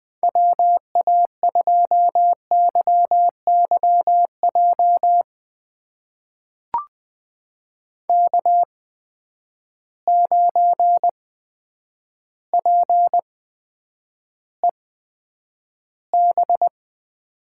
Morse code, heard as a keyed tone, about 20 wpm, spaced as for 9 wpm, 700 Hz. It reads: WA2YYJ K9PEB
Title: USA Call Signs - 8x Character Spacing - 20wpm